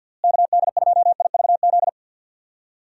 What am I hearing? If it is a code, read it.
Morse code, heard as a keyed tone, about 52 wpm, 700 Hz. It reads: KD2IVZ
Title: USA Call Signs - 6 Characters - Repeated 2x - 50wpm